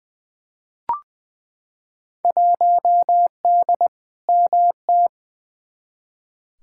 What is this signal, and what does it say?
Morse code, heard as a keyed tone, about 20 wpm, 700 Hz. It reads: 1D MT
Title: Field Day Exchanges - 20wpm